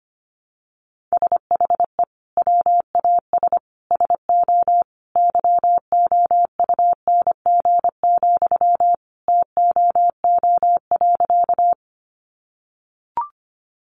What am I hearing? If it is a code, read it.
Morse code, heard as a keyed tone, about 25 wpm, 700 Hz. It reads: SHE WAS SO YOUNG, TOO.
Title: Sentences from Top 400 Words 25wpm